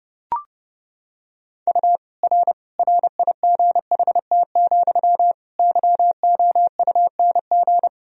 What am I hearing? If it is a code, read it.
Morse code, heard as a keyed tone, about 30 wpm, 700 Hz. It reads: U R RIGHT, YOUNG
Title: Sentences from Top 400 Words and Abbreviations 30wpm